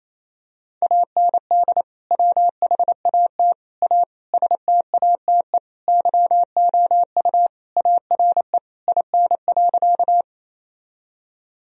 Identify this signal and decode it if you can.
Morse code, heard as a keyed tone, about 28 wpm, 700 Hz. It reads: AND WHAT A STATE YOU ARE IN.
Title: Sentences from Top 1000 Words - Review Part 2 - 28wpm